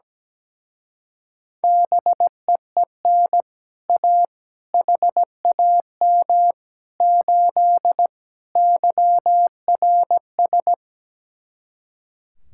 Morse code, heard as a keyed tone, about 17 wpm, 700 Hz. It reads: BEEN A HAM 8 YRS
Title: Speed-Racing Common QSO Phrases - 17wpm